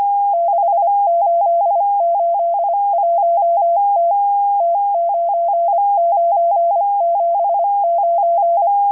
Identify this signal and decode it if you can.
Morse code, heard as a keyed tone, about 25 wpm, 680 Hz, with radio noise. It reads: 6881T T9979